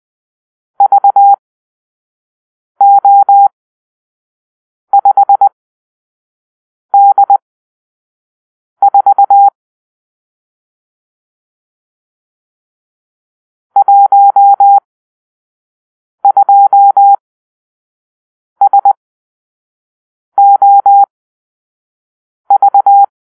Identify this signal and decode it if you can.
Morse code, heard as a keyed tone, about 20 wpm, spaced as for 5 wpm, 800 Hz. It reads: VO5D4 12SOV
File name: arrl / Grp-6 Drill-3